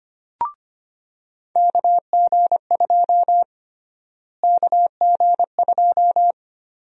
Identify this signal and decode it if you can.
Morse code, heard as a keyed tone, about 25 wpm, 700 Hz. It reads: KG2 KG2
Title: USA Call Sign Prefixes - Repeated 2x - 25wpm